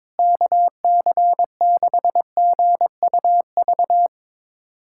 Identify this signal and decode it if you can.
Morse code, heard as a keyed tone, about 22 wpm, 700 Hz. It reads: KC6GUV
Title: USA Callsign DE Callsign 22wpm